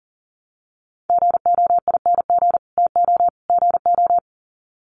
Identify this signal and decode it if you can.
Morse code, heard as a keyed tone, about 40 wpm, 700 Hz. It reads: GOING TO GO
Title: Sets of 3 Words 40wpm